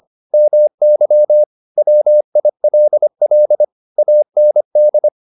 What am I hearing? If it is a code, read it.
Morse code, heard as a keyed tone, about 25 wpm, 600 Hz. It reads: MY WILL AND
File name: the old man and the sea part 1 25wpm